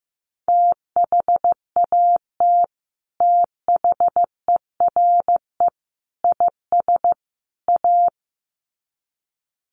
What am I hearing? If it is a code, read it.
Morse code, heard as a keyed tone, about 15 wpm, 700 Hz. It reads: THAT THERE IS A
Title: Sets of 4 Words 15wpm